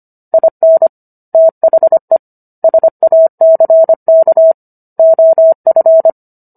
Morse code, heard as a keyed tone, about 25 wpm, 650 Hz. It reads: IN THE SACK OF